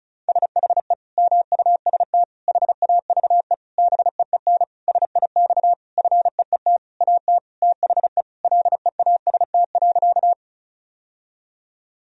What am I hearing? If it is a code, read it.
Morse code, heard as a keyed tone, about 34 wpm, 700 Hz. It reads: SHE MUST HAVE BEEN SIX FEET AT THE LEAST.
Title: Sentences from Top 1000 Words - Review Part 2 - 35wpm